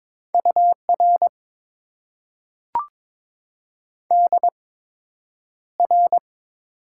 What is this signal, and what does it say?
Morse code, heard as a keyed tone, about 23 wpm, spaced as for 9 wpm, 700 Hz. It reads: UR DR
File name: Top 90 QSO Elements - 8x Character Spacing - 22wpm